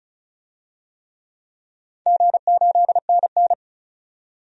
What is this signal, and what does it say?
Morse code, heard as a keyed tone, about 36 wpm, 700 Hz. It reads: G8NN
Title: DXCC-100 Call Signs - 4 Characters - 35wpm